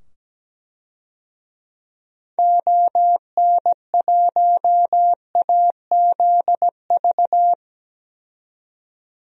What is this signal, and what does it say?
Morse code, heard as a keyed tone, about 17 wpm, 700 Hz. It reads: ON1AZV
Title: DXCC-100 Call Signs - 6 Characters - 17wpm